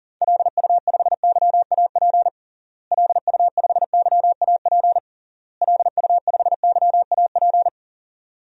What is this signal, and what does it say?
Morse code, heard as a keyed tone, about 37 wpm, 700 Hz. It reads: LU5YAP LU5YAP LU5YAP
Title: DXCC-100 Call Signs - 6 Characters - Repeated 3x - 40wpm